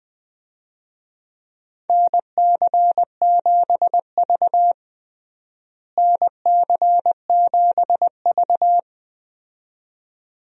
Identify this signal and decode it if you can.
Morse code, heard as a keyed tone, about 21 wpm, 700 Hz. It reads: NC7V NC7V
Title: USA Call Signs - 4 Characters - Repeated 2x - 20wpm